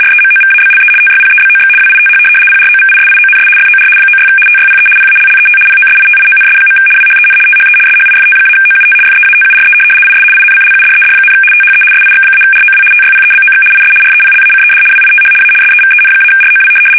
RTTY, about 75 baud. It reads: REMOVABLE TYPEBOX WITH PROVISIONS FOR 96 TYPE PALLET POSITIONS. WHEN THE SHIFT-OUT FEATURE IS INCLUDED, THE SIX-ROW TYPEBOX IS REPLACED WITH A SEVEN-ROW
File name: fsk850-75B / teletype-corp